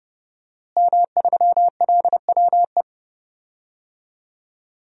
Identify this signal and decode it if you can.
Morse code, heard as a keyed tone, about 30 wpm, 700 Hz. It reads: M3LWE
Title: DXCC-100 Call Signs - 5 Characters - Repeated 2x - 30wpm